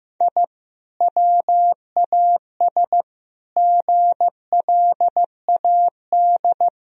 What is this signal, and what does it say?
Morse code, heard as a keyed tone, about 15 wpm, 700 Hz. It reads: I WAS GLAD
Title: Sentences from Top 1000 Words - Review Part 1 - 15wpm